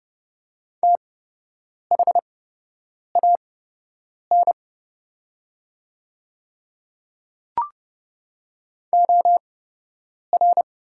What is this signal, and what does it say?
Morse code, heard as a keyed tone, about 30 wpm, spaced as for 8 wpm, 700 Hz. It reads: THAN OR
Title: 8x Character Spacing - Top 100 Words - 30wpm